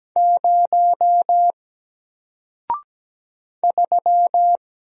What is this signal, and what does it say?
Morse code, heard as a keyed tone, about 17 wpm, 700 Hz. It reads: 0 3
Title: Single Letter-Number Alt 17wpm